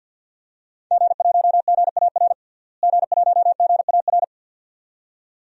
Morse code, heard as a keyed tone, about 46 wpm, 700 Hz. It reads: G1GAR G1GAR
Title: DXCC-100 Call Signs - 5 Characters - Repeated 2x - 50wpm